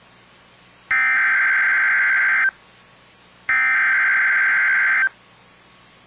Emergency Alert System (SAME) tones (header bursts), heard as harsh buzzing bursts. It is heard from 0.9 to 5.0 s.